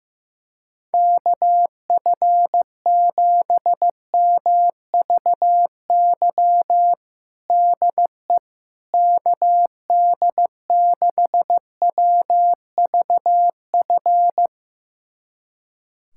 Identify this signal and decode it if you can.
Morse code, heard as a keyed tone, about 15 wpm, 700 Hz. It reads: KF7MVY DE KD6WVF
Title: USA Callsign DE Callsign - 6 Characters - 15wpm